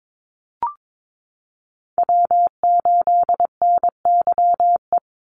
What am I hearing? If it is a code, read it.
Morse code, heard as a keyed tone, about 22 wpm, 700 Hz. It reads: W8NYE
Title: USA Call Signs 22wpm